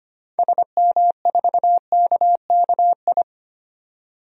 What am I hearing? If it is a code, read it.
Morse code, heard as a keyed tone, about 25 wpm, 700 Hz. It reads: SM4KKI
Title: DXCC-100 Call Signs - 6 Characters - 25wpm